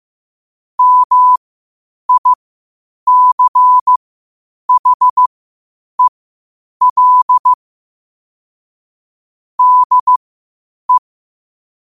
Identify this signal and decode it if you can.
Morse code, heard as a keyed tone, about 15 wpm, spaced as for 8 wpm, 1000 Hz. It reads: MICHEL DE
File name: qotdfiles / 8wpm